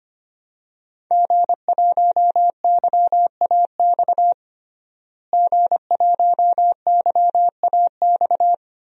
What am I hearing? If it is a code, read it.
Morse code, heard as a keyed tone, about 25 wpm, 700 Hz. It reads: G1YAX G1YAX
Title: DXCC-100 Call Signs - 5 Characters - Repeated 3x - 25wpm